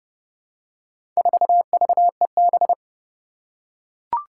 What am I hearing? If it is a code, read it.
Morse code, heard as a keyed tone, about 28 wpm, 700 Hz. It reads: 4VEB
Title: Four Challenging Letter-Number Combinations 30wpm